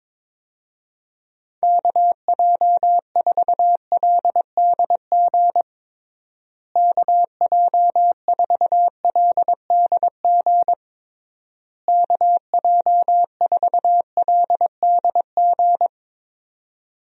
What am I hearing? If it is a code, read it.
Morse code, heard as a keyed tone, about 22 wpm, 700 Hz. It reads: KJ4LDG KJ4LDG KJ4LDG
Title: USA Call Signs - 6 Characters - Repeated 3x - 22wpm